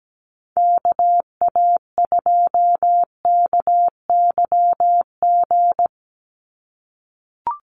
Morse code, heard as a keyed tone, about 17 wpm, 700 Hz. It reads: KA2KYG
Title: USA Call Signs 17wpm